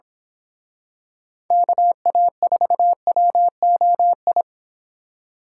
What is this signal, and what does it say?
Morse code, heard as a keyed tone, about 26 wpm, 700 Hz. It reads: KA4WOI
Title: Speed Racing - USA Call Signs 22wpm